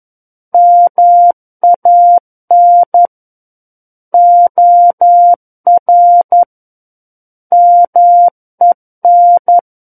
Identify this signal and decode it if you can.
Morse code, heard as a keyed tone, about 11 wpm, 700 Hz. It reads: MAN OR MEN
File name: 11 WPM